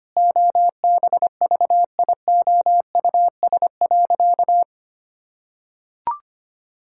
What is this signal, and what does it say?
Morse code, heard as a keyed tone, about 25 wpm, 700 Hz. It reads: OBVIOUS.
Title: Sentences from Top 1600 Words 25wpm